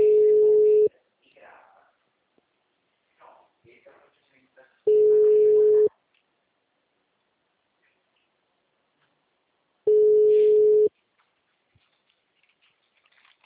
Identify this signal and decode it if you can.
Ringback tone (Europe, 425 Hz), with speech.